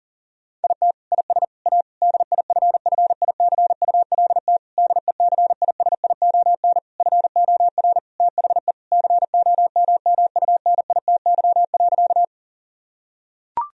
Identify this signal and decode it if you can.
Morse code, heard as a keyed tone, about 40 wpm, 690 Hz. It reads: IT IS A DIFFICULT DECISION FOR THE COMMUNITY.
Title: Sentences from Top 1000 Words - Review Part 4 - 40wpm